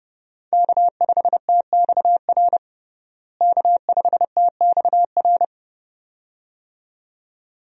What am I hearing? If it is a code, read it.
Morse code, heard as a keyed tone, about 30 wpm, 700 Hz. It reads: K5TXR K5TXR
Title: USA Call Signs - 5 Characters - Repeated 2x - 30wpm